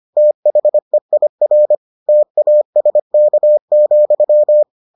Morse code, heard as a keyed tone, about 25 wpm, 600 Hz. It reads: THEIR TASK,